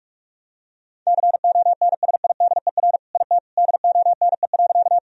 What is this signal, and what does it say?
Morse code, heard as a keyed tone, about 46 wpm, 700 Hz. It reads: CONSIDER IT DONE.
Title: Sentences from Top 1000 Words - Review Part 4 - 45wpm